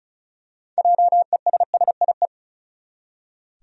Morse code, heard as a keyed tone, about 36 wpm, 700 Hz. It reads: JESSIE
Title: USA Common Names - 35wpm